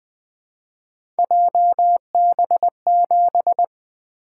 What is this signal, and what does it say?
Morse code, heard as a keyed tone, about 20 wpm, 700 Hz. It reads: JB7
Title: Three Letter-Number Combinations - 20wpm